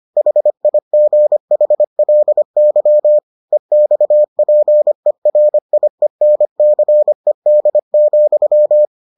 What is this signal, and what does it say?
Morse code, heard as a keyed tone, about 25 wpm, 600 Hz. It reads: HIGHLY EXPERIENCED,